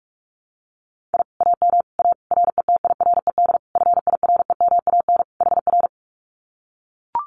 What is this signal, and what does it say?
Morse code, heard as a keyed tone, about 44 wpm, 700 Hz. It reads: I AM A RETIRED FIREMAN HR
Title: Common QSO Phrases Encore 45wpm